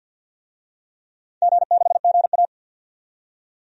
Morse code, heard as a keyed tone, about 52 wpm, 700 Hz. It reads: G6GA
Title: DXCC-100 Call Signs - 4 Characters - 50wpm